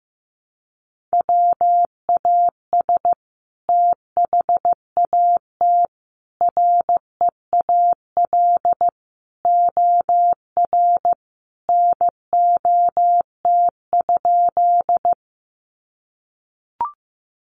Morse code, heard as a keyed tone, about 15 wpm, 700 Hz. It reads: WAS THAT REAL OR NOT?
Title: Sentences from Top 600 Words 15wpm